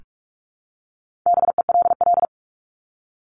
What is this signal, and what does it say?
Morse code, heard as a keyed tone, about 44 wpm, 700 Hz. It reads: BELL
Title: Top 1751-2000 Words 45wpm